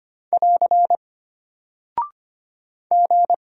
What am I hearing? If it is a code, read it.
Morse code, heard as a keyed tone, about 25 wpm, 700 Hz. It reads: <AR> G
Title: Single Character (Letter:Number:Punctuation:Prosign) Alt 25wpm